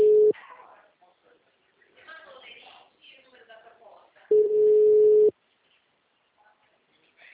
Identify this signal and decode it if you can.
Ringback tone (Europe, 425 Hz), with speech.